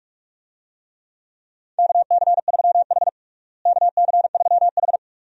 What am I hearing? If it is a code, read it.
Morse code, heard as a keyed tone, about 46 wpm, 700 Hz. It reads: KC3H KC3H
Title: USA Call Signs - 4 Characters - Repeated 2x - 45wpm